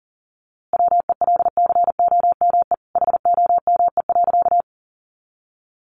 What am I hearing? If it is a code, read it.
Morse code, heard as a keyed tone, about 40 wpm, 700 Hz. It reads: WELCOME HOME.
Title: A Sound Of Thunder 40wpm